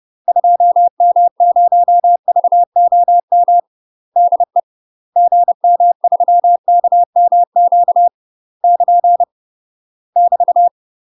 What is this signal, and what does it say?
Morse code, heard as a keyed tone, about 30 wpm, 700 Hz. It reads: 2M0VOM DE GM3KMQ <KN> <BT>
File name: UK MASTER QSOs TRAINING / 30WPM / UK MASTER QSOs 0036